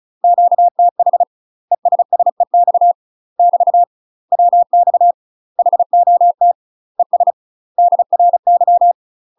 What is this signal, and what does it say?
Morse code, heard as a keyed tone, about 34 wpm, 700 Hz. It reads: QTH ESSEX <BT> WX HOT ES DRY
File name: UK MASTER QSOs TRAINING / 35WPM / UK MASTER QSOs 0008